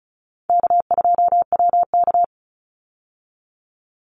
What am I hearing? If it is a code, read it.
Morse code, heard as a keyed tone, about 34 wpm, 700 Hz. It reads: K2WK
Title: Easy Call Signs Repeated 2x - 35wpm